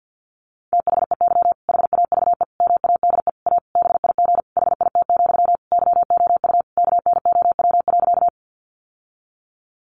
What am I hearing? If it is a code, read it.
Morse code, heard as a keyed tone, about 50 wpm, 700 Hz. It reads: THEY HAVE MADE A BIG HIT, YOU KNOW.
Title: Sentences from Top 1200 Words 50wpm